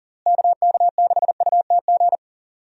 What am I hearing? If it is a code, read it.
Morse code, heard as a keyed tone, about 40 wpm, 700 Hz. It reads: KK6UTG
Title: USA Call Signs - 6 Characters - Repeated 3x - 40wpm